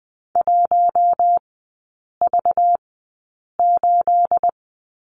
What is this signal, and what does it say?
Morse code, heard as a keyed tone, about 20 wpm, spaced as for 8 wpm, 690 Hz. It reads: 1V8
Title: 212 Lesson 34e - Number 8 - Three Characters 20wpm